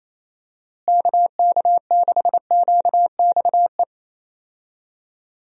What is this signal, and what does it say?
Morse code, heard as a keyed tone, about 28 wpm, 700 Hz. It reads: KK6QXE
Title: USA Call Signs - 6 Characters - Repeated 2x - 28wpm